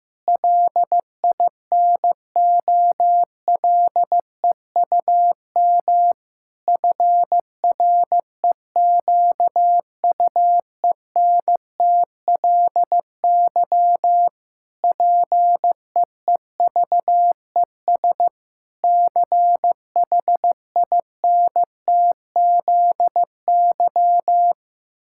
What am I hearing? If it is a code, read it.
Morse code, heard as a keyed tone, about 15 wpm, 700 Hz. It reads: LINOLEUM FREQUENTLY PEEVES CHINTZY
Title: Pangram Challenge v2 - 15wpm